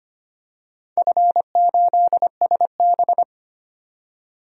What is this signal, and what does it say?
Morse code, heard as a keyed tone, about 25 wpm, 700 Hz. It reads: F8SB
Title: DXCC-100 Call Signs - 4 Characters - 25wpm